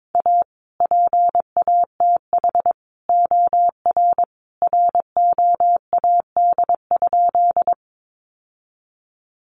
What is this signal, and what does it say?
Morse code, heard as a keyed tone, about 22 wpm, 700 Hz. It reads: A PATH OR ROAD?
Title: Sentences from Top 1400 Words 22wpm